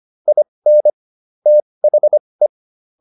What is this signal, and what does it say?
Morse code, heard as a keyed tone, about 25 wpm, spaced as for 20 wpm, 600 Hz. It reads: IN THE